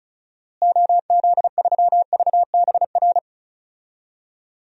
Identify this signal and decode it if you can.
Morse code, heard as a keyed tone, about 34 wpm, 700 Hz. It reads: OZ3VBR
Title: DXCC-100 Call Signs - 6 Characters - Repeated 2x - 35wpm